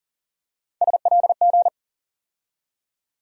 Morse code, heard as a keyed tone, about 39 wpm, 700 Hz. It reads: SLG
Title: USA Call Sign Postfixes - 3 Letters 40wpm